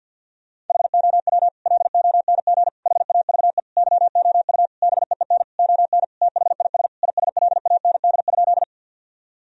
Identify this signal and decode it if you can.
Morse code, heard as a keyed tone, about 50 wpm, 700 Hz. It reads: HOW LONG HAVE YOU BEEN ON THIS ISLAND?
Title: Sentences from Top 1000 Words - Review Part 3 - 50wpm